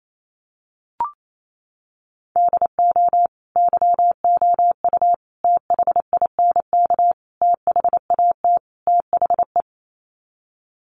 Morse code, heard as a keyed tone, about 28 wpm, 700 Hz. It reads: DO YOU THINK THAT THE
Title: Sets of 5 Words Encore 28wpm